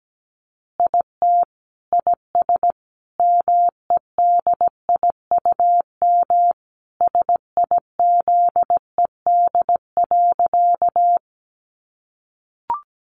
Morse code, heard as a keyed tone, about 17 wpm, 700 Hz. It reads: IT IS MEDIUM SIZED.
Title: Easy Phrases 17wpm